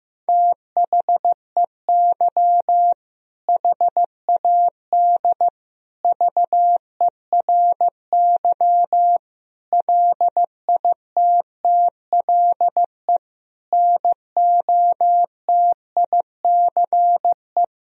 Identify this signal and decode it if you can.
Morse code, heard as a keyed tone, about 15 wpm, 700 Hz. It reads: THEY HAD VERY LITTLE NOTICE
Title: Sentences from Top 1000 Words - Review Part 2 - 15wpm